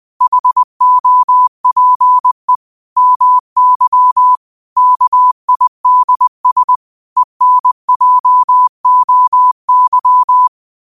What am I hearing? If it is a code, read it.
Morse code, heard as a keyed tone, about 20 wpm, 1000 Hz. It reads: HOPE MY KIDS ENJOY
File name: qotdfiles / 20wpm